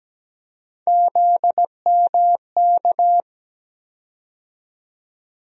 Morse code, heard as a keyed tone, about 17 wpm, 700 Hz. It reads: ZMK
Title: USA Call Sign Postfixes - 3 Letters 17wpm